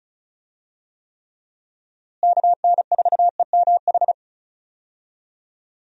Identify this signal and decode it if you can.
Morse code, heard as a keyed tone, about 36 wpm, 700 Hz. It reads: KN4EMH
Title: USA Call Signs - 6 Characters - Repeated 3x - 35wpm